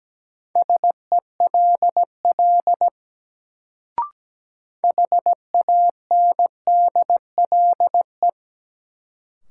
Morse code, heard as a keyed tone, about 17 wpm, 700 Hz. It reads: SELL HANDLE
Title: Sentences from Top 1100 Words - New Word Drill Alt - 17wpm